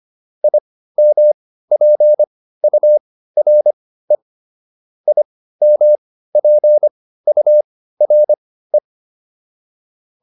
Morse code, heard as a keyed tone, about 25 wpm, spaced as for 15 wpm, 600 Hz. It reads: IMPURE IMPURE
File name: IM6 15